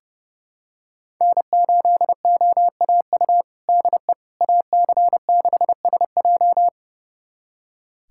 Morse code, heard as a keyed tone, about 30 wpm, 700 Hz. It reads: N8OAU DE AC6SJ
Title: USA Callsign DE Callsign - 5 Characters - 30wpm